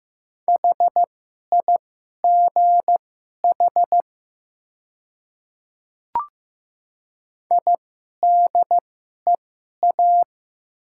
Morse code, heard as a keyed tone, about 15 wpm, spaced as for 11 wpm, 700 Hz. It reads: HIGH IDEA
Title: 2x Character Spacing - Top 300 Words - 15wpm